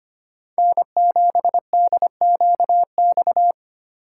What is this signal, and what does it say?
Morse code, heard as a keyed tone, about 30 wpm, 700 Hz. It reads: N7DQX
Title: Speed Racing - USA 1x3 Call Signs 25wpm